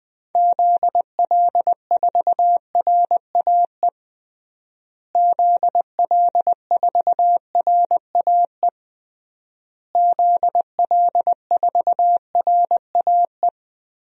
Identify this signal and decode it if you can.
Morse code, heard as a keyed tone, about 20 wpm, 700 Hz. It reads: ZL4RAE ZL4RAE ZL4RAE
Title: DXCC-100 Call Signs - 6 Characters - Repeated 3x - 20wpm